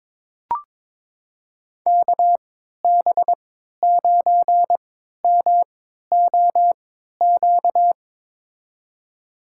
Morse code, heard as a keyed tone, about 21 wpm, spaced as for 12 wpm, 700 Hz. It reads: KB9MOQ
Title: USA Call Signs - 6 Characters - 3x Character Spacing - 22wpm